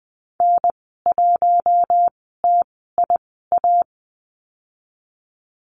Morse code, heard as a keyed tone, about 20 wpm, spaced as for 15 wpm, 700 Hz. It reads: N1TIA
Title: 055 Lesson 10g - Letter r - Call Signs 20wpm